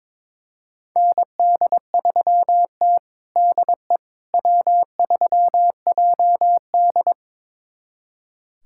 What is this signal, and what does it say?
Morse code, heard as a keyed tone, about 22 wpm, 700 Hz. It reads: ND3T DE W3JD
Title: USA Callsign DE Callsign - 4 Characters - 22wpm